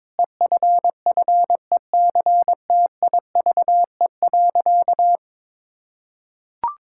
Morse code, heard as a keyed tone, about 22 wpm, 700 Hz. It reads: EFFECTIVE.
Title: Sentences from Top 1000 Words - Review Part 6 - 22wpm